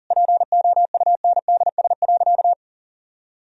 Morse code, heard as a keyed tone, about 40 wpm, 700 Hz. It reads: POUNDS.
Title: Sentences from Top 1000 Words - Review Part 2 - 40wpm